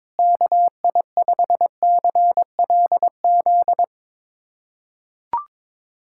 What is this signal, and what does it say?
Morse code, heard as a keyed tone, about 22 wpm, 700 Hz. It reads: KI5CLZ
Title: USA Callsign DE Callsign - 6 Characters - 22wpm